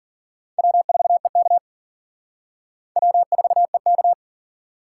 Morse code, heard as a keyed tone, about 53 wpm, 700 Hz. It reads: W4EK W4EK